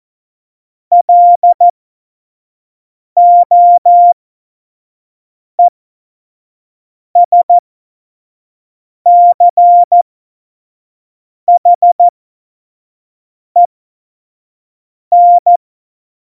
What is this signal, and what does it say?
Morse code, heard as a keyed tone, about 14 wpm, spaced as for 5 wpm, 700 Hz. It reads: LOESCHEN
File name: Morsekurs / Witze / Witz 04 Juli 17, Tempo 25 F70